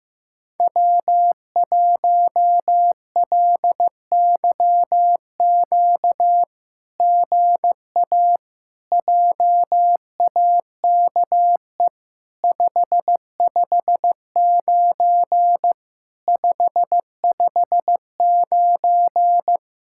Morse code, heard as a keyed tone, about 14 wpm, 700 Hz. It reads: W1LYQ GA JAKE 559 559
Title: SOTA QSO Practice 15wpm